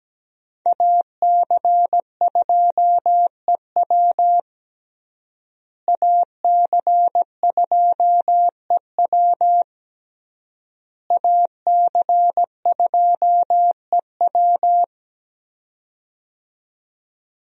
Morse code, heard as a keyed tone, about 17 wpm, 700 Hz. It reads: AC2EW AC2EW AC2EW
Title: USA Call Signs - 5 Characters - Repeated 3x - 17wpm